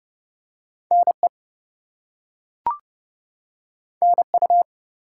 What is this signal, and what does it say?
Morse code, heard as a keyed tone, about 30 wpm, 700 Hz. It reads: NE NU
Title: US States and Canadian Provinces 30wpm